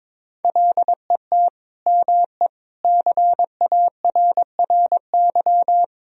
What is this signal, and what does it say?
Morse code, heard as a keyed tone, about 22 wpm, 700 Hz. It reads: LET ME CARRY
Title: Sentences from Top 1000 Words - Review Part 1 - 22wpm